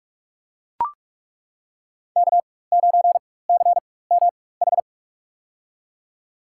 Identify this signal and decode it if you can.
Morse code, heard as a keyed tone, about 42 wpm, spaced as for 21 wpm, 700 Hz. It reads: K9CMH
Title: USA Call Signs - 5 Characters - 4x Character Spacing - 45wpm